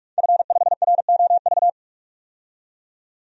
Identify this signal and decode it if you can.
Morse code, heard as a keyed tone, about 46 wpm, 700 Hz. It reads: F5ROV
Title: DXCC-100 Call Signs - 5 Characters - 45wpm